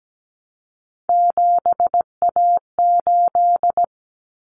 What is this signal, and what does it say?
Morse code, heard as a keyed tone, about 17 wpm, 690 Hz. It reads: 7A8
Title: Common Cut Numbers - 3-Digit 17wpm